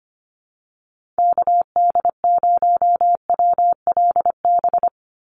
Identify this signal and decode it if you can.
Morse code, heard as a keyed tone, about 25 wpm, 700 Hz. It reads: KD0WLB
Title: USA Call Signs 25wpm